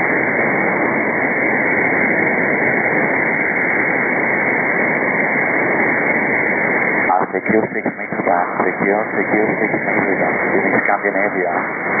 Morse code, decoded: none